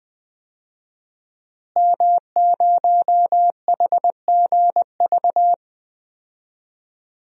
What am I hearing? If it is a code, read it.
Morse code, heard as a keyed tone, about 20 wpm, 700 Hz. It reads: M0HGV
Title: DXCC-100 Call Signs - 5 Characters - Repeated 3x - 20wpm